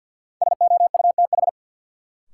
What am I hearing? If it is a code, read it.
Morse code, heard as a keyed tone, about 47 wpm, 700 Hz. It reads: SOUTH
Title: Sentences from Top 400 Words - New Word Drill - 50wpm